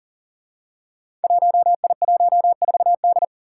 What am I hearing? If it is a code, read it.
Morse code, heard as a keyed tone, about 40 wpm, 700 Hz. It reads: 1I14D
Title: Five Challenging Letter-Number Combinations 40wpm